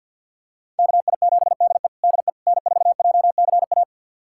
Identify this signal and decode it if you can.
Morse code, heard as a keyed tone, about 49 wpm, 700 Hz. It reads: KI7DE DE N4JCA
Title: USA Callsign DE Callsign - 5 Characters - 50wpm